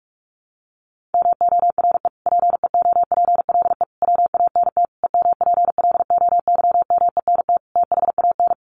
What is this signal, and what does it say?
Morse code, heard as a keyed tone, about 44 wpm, 700 Hz. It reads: MORE PEOPLE WANT EMPLOYMENT THAN
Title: Sentences from Top 1500 Words 45wpm